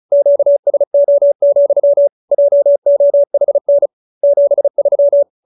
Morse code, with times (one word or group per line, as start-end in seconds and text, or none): 0.1-2.1 QSO,
2.3-3.9 JOHN
4.2-5.2 73